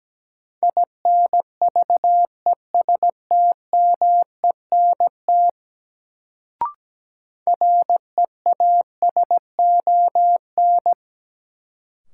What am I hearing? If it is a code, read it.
Morse code, heard as a keyed tone, about 17 wpm, 700 Hz. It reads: INVESTMENT REASON
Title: Sentences from Top 700 Words - New Word Drill - 17wpm